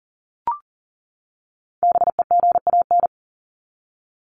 Morse code, heard as a keyed tone, about 40 wpm, 700 Hz. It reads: BEGAN
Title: Top 300 Words 40wpm